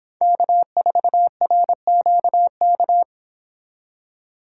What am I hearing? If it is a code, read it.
Morse code, heard as a keyed tone, about 25 wpm, 700 Hz. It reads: K4RQK